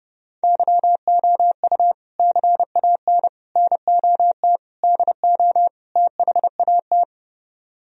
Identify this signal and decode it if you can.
Morse code, heard as a keyed tone, about 37 wpm, 700 Hz. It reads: YOU CAN NOT DO THAT
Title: Speed-Racing - Sets of 5 Words 30wpm